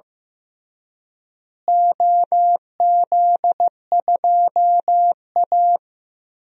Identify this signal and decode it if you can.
Morse code, heard as a keyed tone, about 15 wpm, 700 Hz. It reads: OZ2A
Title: DXCC-100 Call Signs - 4 Characters - Repeated 3x - 15wpm